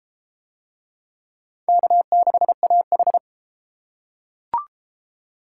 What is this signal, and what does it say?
Morse code, heard as a keyed tone, about 31 wpm, 700 Hz. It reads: K6AH